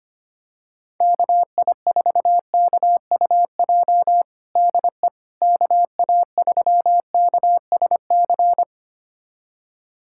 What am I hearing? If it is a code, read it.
Morse code, heard as a keyed tone, about 25 wpm, 700 Hz. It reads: KI4KUJ DE KA3KSC
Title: USA Callsign DE Callsign 25wpm